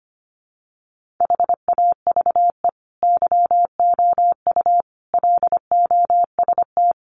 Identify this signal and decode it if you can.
Morse code, heard as a keyed tone, about 25 wpm, 700 Hz. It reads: HAVE YOU LOST
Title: Sentences from Top 1400 Words 25wpm